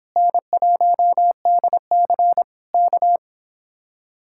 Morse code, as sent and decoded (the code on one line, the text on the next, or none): -. .---- -.. -.-. / -.-
N1DC K